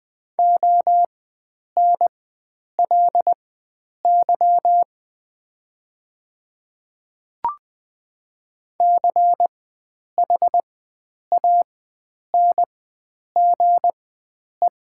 Morse code, heard as a keyed tone, about 20 wpm, spaced as for 9 wpm, 700 Hz. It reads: ONLY CHANGE
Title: 4x Character Spacing - Top 300 Words - 20wpm